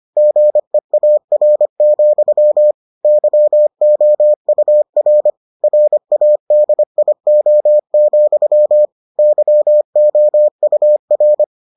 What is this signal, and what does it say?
Morse code, heard as a keyed tone, about 25 wpm, 600 Hz. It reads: GEAR, YOUR RADIO, YOUR